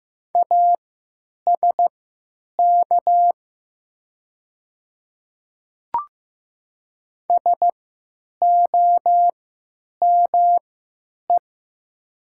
Morse code, heard as a keyed tone, about 15 wpm, spaced as for 9 wpm, 700 Hz. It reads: ASK SOME
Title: 3x Character Spacing - Top 200 Words - 15wpm